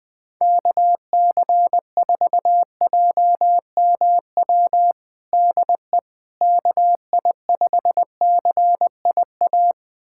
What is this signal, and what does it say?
Morse code, heard as a keyed tone, about 20 wpm, 700 Hz. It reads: KC4JMW DE KI5CIA
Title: USA Callsign DE Callsign 20wpm